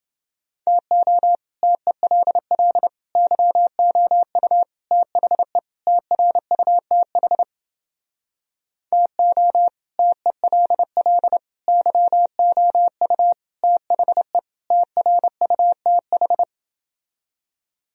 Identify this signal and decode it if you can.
Morse code, heard as a keyed tone, about 30 wpm, 700 Hz. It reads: TO TELL YOU THE TRUTH TO TELL YOU THE TRUTH